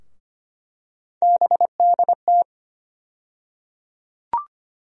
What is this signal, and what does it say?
Morse code, heard as a keyed tone, about 25 wpm, 700 Hz. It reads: BDT
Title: USA Call Sign Postfixes - 3 Letters 25wpm